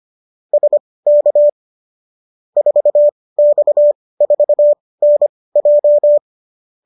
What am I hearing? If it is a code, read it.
Morse code, heard as a keyed tone, about 25 wpm, spaced as for 18 wpm, 590 Hz. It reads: SK 4X4NJ